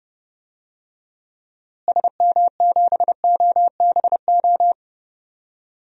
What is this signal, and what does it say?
Morse code, heard as a keyed tone, about 30 wpm, 700 Hz. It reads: SM7OBO
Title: DXCC-100 Call Signs - 6 Characters - Repeated 3x - 30wpm